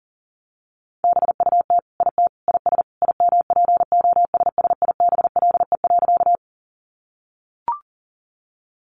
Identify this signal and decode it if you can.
Morse code, heard as a keyed tone, about 40 wpm, 700 Hz. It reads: BUT IT IS IMPOSSIBLE.
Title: Sentences from Top 1100 Words 40wpm